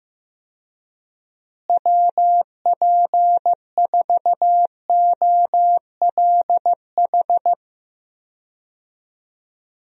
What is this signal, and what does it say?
Morse code, heard as a keyed tone, about 15 wpm, 700 Hz. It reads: WP4OLH
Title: USA Call Signs - 6 Characters - Repeated 3x - 15wpm